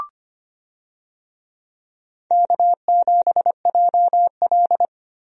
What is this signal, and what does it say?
Morse code, heard as a keyed tone, about 25 wpm, 700 Hz. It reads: K7JL
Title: USA Call Signs - 4 Characters - Repeated 2x - 25wpm